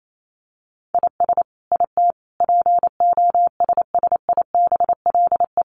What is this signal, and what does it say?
Morse code, heard as a keyed tone, about 28 wpm, 700 Hz. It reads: IS IT POSSIBLE
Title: Sentences from Top 1200 Words 28wpm